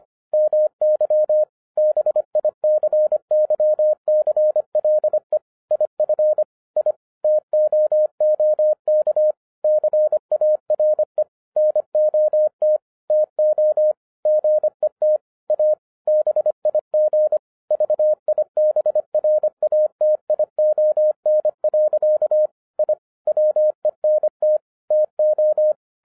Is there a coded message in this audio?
Yes: Morse code from 0.3 to 25.7 s, heard as a keyed tone, about 25 wpm, 610 Hz. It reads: MY BICYCLE IF I TOOK CARE NOT TO GET A BIG VIBRATION. I WENT TO